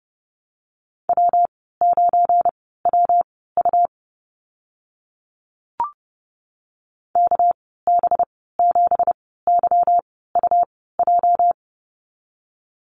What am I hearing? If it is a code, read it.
Morse code, heard as a keyed tone, about 32 wpm, spaced as for 17 wpm, 700 Hz. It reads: W9WU KB7YUJ
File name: Lesson 37g - Letter x - Call Signs 30wpm